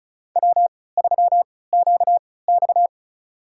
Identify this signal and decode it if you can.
Morse code, heard as a keyed tone, about 34 wpm, spaced as for 20 wpm, 700 Hz. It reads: W3QX
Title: USA Call Signs - 3x Character Spacing - 35wpm